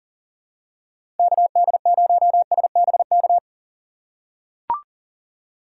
Morse code, heard as a keyed tone, about 40 wpm, 700 Hz. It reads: KD0SBK
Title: USA Call Signs - 6 Characters - 2x Character Spacing - 40wpm